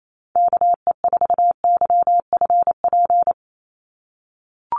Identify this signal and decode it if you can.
Morse code, heard as a keyed tone, about 28 wpm, 700 Hz. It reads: KE4YFP
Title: USA Call Signs 28wpm